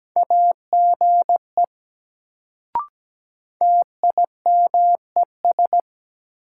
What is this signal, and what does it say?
Morse code, heard as a keyed tone, about 17 wpm, 700 Hz. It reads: AGE TIMES
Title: Sets of 3 Words - New Word Drill - 17wpm